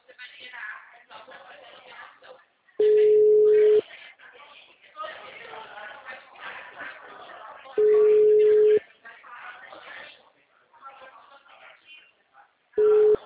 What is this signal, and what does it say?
Ringback tone (Europe, 425 Hz), with speech.